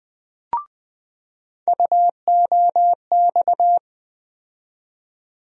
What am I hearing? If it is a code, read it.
Morse code, heard as a keyed tone, about 20 wpm, 700 Hz. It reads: UOX
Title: USA Call Sign Postfixes - 3 Letters 20wpm